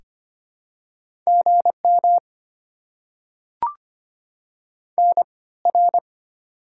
Morse code, heard as a keyed tone, about 25 wpm, 690 Hz. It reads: GM N R
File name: Top 90 QSO Elements - 3x Character Spacing - 25wpm